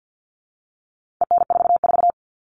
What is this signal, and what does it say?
Morse code, heard as a keyed tone, about 50 wpm, 700 Hz. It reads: EN44
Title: All Cut Numbers - 4-Digit 50wpm